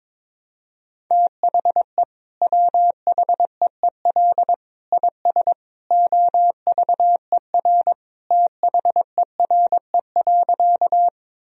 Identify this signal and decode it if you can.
Morse code, heard as a keyed tone, about 22 wpm, 700 Hz. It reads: THE WHEEL IS OVER THERE.
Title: Sentences from Top 1000 Words - Review Part 3 - 22wpm